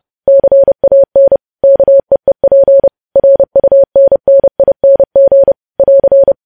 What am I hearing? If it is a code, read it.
Morse code, heard as a keyed tone, about 30 wpm, 550 Hz. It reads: CAN KEEP RUNNING <AR>
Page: news headlines in morse code at 30 wpm